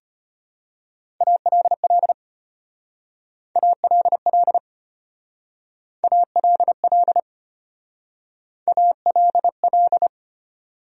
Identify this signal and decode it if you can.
Morse code, heard as a keyed tone, about 30 wpm, 700 Hz. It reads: ALL ALL ALL ALL
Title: Speed-Racing Top 100 Words 25wpm